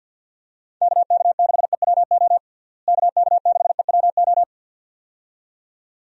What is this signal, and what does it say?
Morse code, heard as a keyed tone, about 48 wpm, 700 Hz. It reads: KK6EWO KK6EWO
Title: USA Call Signs - 6 Characters - Repeated 2x - 50wpm